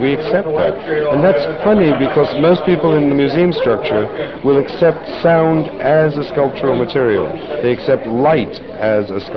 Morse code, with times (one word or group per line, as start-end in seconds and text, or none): none